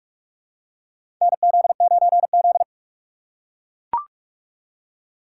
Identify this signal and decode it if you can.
Morse code, heard as a keyed tone, about 44 wpm, 690 Hz. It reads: NZ9Z
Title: USA Call Signs - 4 Characters - 4x Character Spacing - 45wpm